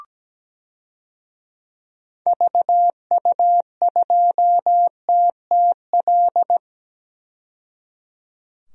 Morse code, heard as a keyed tone, about 17 wpm, 700 Hz. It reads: VU2TTL
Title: DXCC-100 Call Signs - 6 Characters - 17wpm